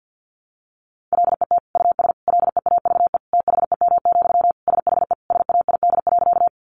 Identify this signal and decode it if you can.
Morse code, heard as a keyed tone, about 50 wpm, 700 Hz. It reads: LET US LEAVE THEM, SHE SAID.
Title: Sentences from Top 400 Words 50wpm